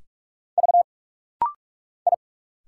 Morse code, heard as a keyed tone, about 46 wpm, 700 Hz. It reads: V I
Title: Single Letters - Rapid-Fire 45wpm